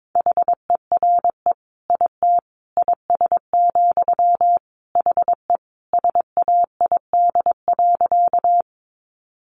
Morse code, heard as a keyed tone, about 22 wpm, 700 Hz. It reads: HERE IT IS, HE SAID.
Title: Sentences from Top 200 Words 22wpm